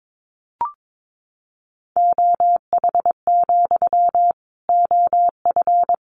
Morse code, heard as a keyed tone, about 22 wpm, 700 Hz. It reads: OH, OF
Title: Sentences from Top 500 Words 22wpm